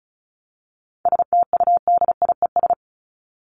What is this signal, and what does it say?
Morse code, heard as a keyed tone, about 34 wpm, 700 Hz. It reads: STUDIES
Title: ES:S Suffix - Top 100 Words with Base Words 35wpm